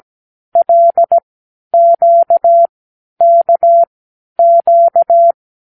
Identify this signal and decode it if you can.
Morse code, heard as a keyed tone, about 17 wpm, spaced as for 10 wpm, 680 Hz. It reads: LQKQ
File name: k7qo / Lesson 36 (A-X)